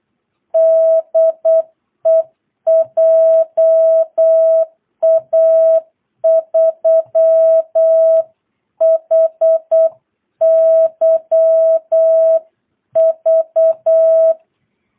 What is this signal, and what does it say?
Morse code, heard as a keyed tone, about 8 wpm, 650 Hz. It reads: DE JA3HYV
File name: jh8cbh / 2022cw